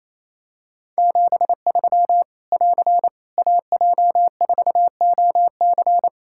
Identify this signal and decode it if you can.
Morse code, heard as a keyed tone, about 28 wpm, 700 Hz. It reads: 73 <AR> AJ4OC
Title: Practice QSO - Exchange 4 - Ending - 28wpm